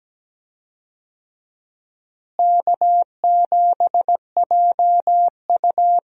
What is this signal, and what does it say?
Morse code, heard as a keyed tone, about 17 wpm, 700 Hz. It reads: K7JU
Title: USA Call Signs - 4 Characters - Repeated 2x - 17wpm